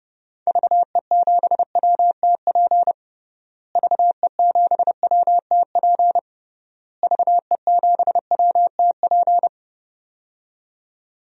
Morse code, heard as a keyed tone, about 30 wpm, 700 Hz. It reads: VE7WTP VE7WTP VE7WTP
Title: DXCC-100 Call Signs - 6 Characters - Repeated 3x - 30wpm